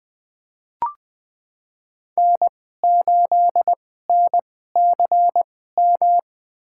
Morse code, heard as a keyed tone, about 21 wpm, spaced as for 15 wpm, 700 Hz. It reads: N8NCM
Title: USA Call Signs - 5 Characters - 2x Character Spacing - 20wpm